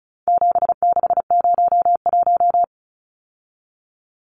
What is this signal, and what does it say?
Morse code, heard as a keyed tone, about 34 wpm, 700 Hz. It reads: 7601
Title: Numbers - 4-Digit 35wpm